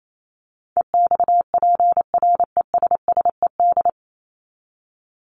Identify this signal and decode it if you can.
Morse code, heard as a keyed tone, about 28 wpm, 700 Hz. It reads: EXPRESSED
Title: ED Suffix - Top 100 Words with Base Words 28wpm